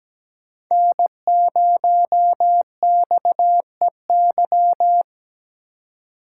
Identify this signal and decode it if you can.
Morse code, heard as a keyed tone, about 17 wpm, 700 Hz. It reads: N0XEY
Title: USA Call Signs - 5 Characters - 4x Character Spacing - 17wpm